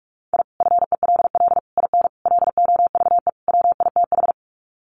Morse code, heard as a keyed tone, about 44 wpm, 700 Hz. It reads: I FELL IN LOVE WITH
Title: Sets of 5 Words Encore 45wpm